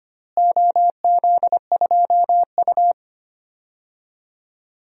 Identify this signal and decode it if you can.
Morse code, heard as a keyed tone, about 25 wpm, 700 Hz. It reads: OZ2U
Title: DXCC-100 Call Signs - 4 Characters - Repeated 2x - 25wpm